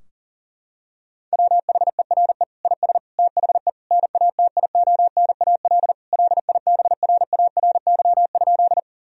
Morse code, heard as a keyed tone, about 40 wpm, 700 Hz. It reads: WHERE IS THE NATIONAL LIBRARY?
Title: Sentences from Top 1000 Words - Review Part 4 - 40wpm